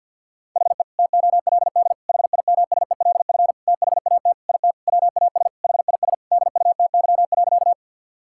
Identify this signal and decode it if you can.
Morse code, heard as a keyed tone, about 50 wpm, 700 Hz. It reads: HE TOLD HIMSELF THAT IT WAS HIS DUTY.
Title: Sentences from Top 1600 Words 50wpm